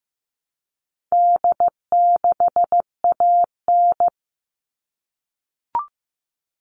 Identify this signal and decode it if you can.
Morse code, heard as a keyed tone, about 15 wpm, 700 Hz. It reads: D6AN
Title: All Cut Numbers - 4-Digit 15wpm